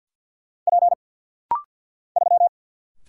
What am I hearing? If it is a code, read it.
Morse code, heard as a keyed tone, about 48 wpm, 700 Hz. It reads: P 3
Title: Single Letter-Number - Rapid-Fire 50wpm